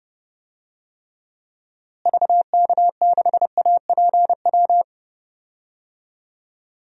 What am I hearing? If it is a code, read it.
Morse code, heard as a keyed tone, about 31 wpm, 700 Hz. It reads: VK6APW
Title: DXCC-100 Call Signs - 6 Characters - Repeated 3x - 30wpm